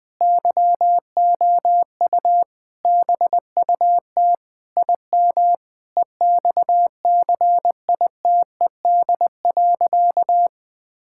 Morse code, heard as a keyed tone, about 20 wpm, 700 Hz. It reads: YOU BUT IM EXCITED.
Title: Sentences with 6-Word Phrases - 20wpm